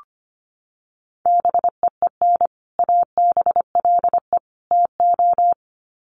Morse code, heard as a keyed tone, about 25 wpm, 700 Hz. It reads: BEEN ABLE TO
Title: Sets of 3 Words 25wpm